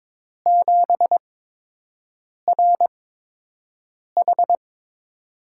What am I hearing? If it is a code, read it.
Morse code, heard as a keyed tone, about 22 wpm, spaced as for 6 wpm, 700 Hz. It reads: 7RH
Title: USA Call Signs - 8x Character Spacing - 22wpm